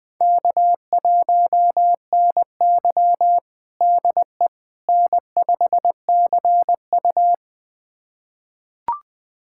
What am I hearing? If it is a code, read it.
Morse code, heard as a keyed tone, about 20 wpm, 700 Hz. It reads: K1NY DE N5CU
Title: USA Callsign DE Callsign - 4 Characters - 20wpm